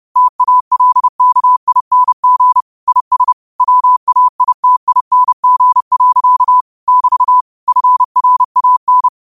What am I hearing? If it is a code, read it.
Morse code, heard as a keyed tone, about 30 wpm, 1000 Hz. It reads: TALKING IS WAITING. <BT> FRAN
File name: 30wpm